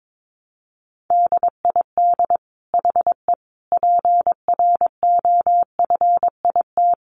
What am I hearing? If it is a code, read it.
Morse code, heard as a keyed tone, about 22 wpm, 700 Hz. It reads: DID HE PROFIT
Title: Sentences from Top 800 Words 22wpm